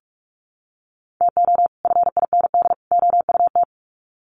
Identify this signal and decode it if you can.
Morse code, heard as a keyed tone, about 44 wpm, 700 Hz. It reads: TO FIND OUT
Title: Sets of 3 Words 45wpm